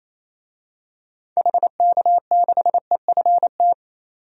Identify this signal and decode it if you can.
Morse code, heard as a keyed tone, about 28 wpm, 700 Hz. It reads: HK6EFT
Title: DXCC-100 Call Signs - 6 Characters - 28wpm